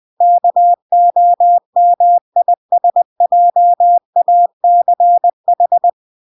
Morse code, heard as a keyed tone, about 20 wpm, 700 Hz. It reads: KOMISJACH